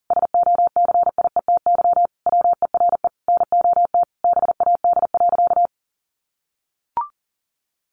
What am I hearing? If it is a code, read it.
Morse code, heard as a keyed tone, about 40 wpm, 700 Hz. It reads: SOCIETY WERE NOT BAD.
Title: Sentences from Top 900 Words 40wpm